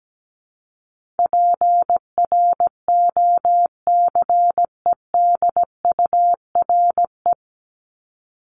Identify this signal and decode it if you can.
Morse code, heard as a keyed tone, about 17 wpm, 700 Hz. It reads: PROCEDURE
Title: Top 1751-2000 Words 17wpm